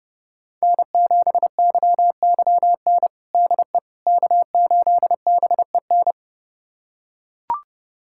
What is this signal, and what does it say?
Morse code, heard as a keyed tone, about 30 wpm, 700 Hz. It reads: N7YYN DE K8BEN
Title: USA Callsign DE Callsign - 5 Characters - 30wpm